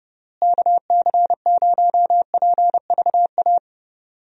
Morse code, heard as a keyed tone, about 30 wpm, 700 Hz. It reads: KC0PVA
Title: USA Call Signs - 6 Characters - Repeated 2x - 30wpm